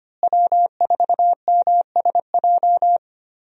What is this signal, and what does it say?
Morse code, heard as a keyed tone, about 25 wpm, 690 Hz. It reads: W4MSJ
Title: USA Call Signs - 5 Characters - Repeated 2x - 25wpm